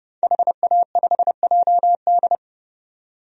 Morse code, heard as a keyed tone, about 30 wpm, 700 Hz. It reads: HA5JD
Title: DXCC-100 Call Signs - 5 Characters - 30wpm